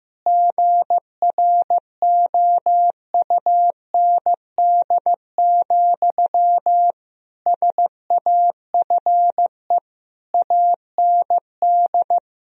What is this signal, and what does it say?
Morse code, heard as a keyed tone, about 15 wpm, 700 Hz. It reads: GROUND, SAFE AND
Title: Sentences from Top 1000 Words - Review Part 2 - 15wpm